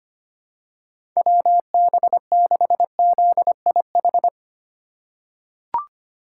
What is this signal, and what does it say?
Morse code, heard as a keyed tone, about 25 wpm, 700 Hz. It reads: WB6ZIH
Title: USA Call Signs - 6 Characters - 8x Character Spacing - 25wpm